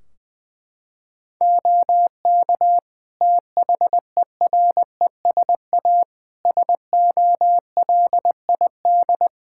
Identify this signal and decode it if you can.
Morse code, heard as a keyed tone, about 20 wpm, 700 Hz. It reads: OK THERESA SOLID
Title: Practice QSO - Exchange 3 - More Info - 20wpm